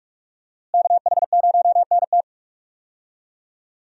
Morse code, heard as a keyed tone, about 43 wpm, 700 Hz. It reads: KH0NT
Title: USA Call Signs - 5 Characters - Repeated 2x - 45wpm